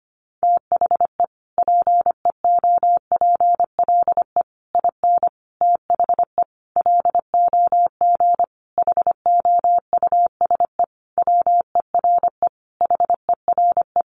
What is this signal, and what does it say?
Morse code, heard as a keyed tone, about 25 wpm, 700 Hz. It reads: THE PEOPLE IN THE LOG HOUSE WERE HERE
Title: Sentences from Top 1400 Words 25wpm